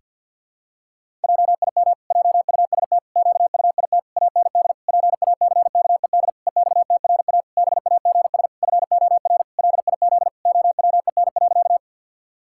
Morse code, heard as a keyed tone, about 51 wpm, 700 Hz. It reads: JIM JUST QUIT AND PACKED EXTRA BAGS FOR LIZ OWEN.
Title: Pangram Challenge v2 - 50wpm